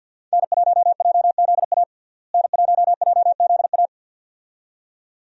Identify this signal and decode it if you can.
Morse code, heard as a keyed tone, about 51 wpm, 700 Hz. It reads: N1JZA N1JZA
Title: USA Call Signs - 5 Characters - Repeated 2x - 50wpm